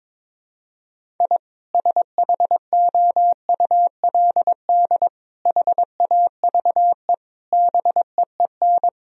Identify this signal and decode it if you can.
Morse code, heard as a keyed tone, about 22 wpm, 700 Hz. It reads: I SHOULD HAVE BEEN
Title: Sentences from Top 1000 Words - Review Part 1 - 22wpm